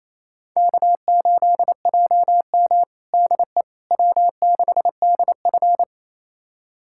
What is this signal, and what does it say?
Morse code, heard as a keyed tone, about 27 wpm, 700 Hz. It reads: K8JM DE W6DF
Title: USA Callsign DE Callsign - 4 Characters - 28wpm